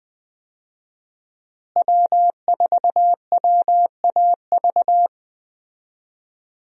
Morse code, heard as a keyed tone, about 20 wpm, 700 Hz. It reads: W4WAV
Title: USA Call Signs - 5 Characters - Repeated 2x - 20wpm